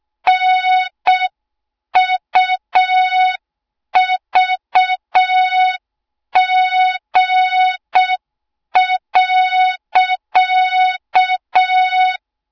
Morse code, heard as a keyed tone, about 6 wpm, 3700 Hz. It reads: NUVG.
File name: Morze-HA2MN / hang / 20 GYAKORLAT